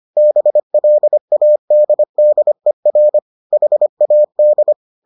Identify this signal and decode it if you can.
Morse code, heard as a keyed tone, about 25 wpm, 600 Hz. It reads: BLADDER HAD